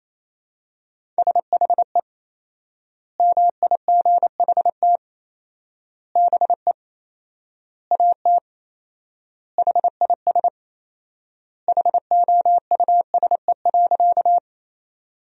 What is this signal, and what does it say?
Morse code, heard as a keyed tone, about 28 wpm, 700 Hz. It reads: SHE MIGHT BE AT HIS HOUSE.
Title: 4x Word-Spacing - Sentences from Top 300 Words - 28wpm